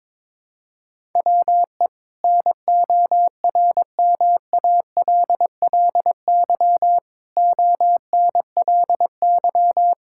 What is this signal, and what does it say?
Morse code, heard as a keyed tone, about 22 wpm, 700 Hz. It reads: WE NORMALLY ONLY
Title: Sentences from Top 1000 Words - Review Part 6 - 22wpm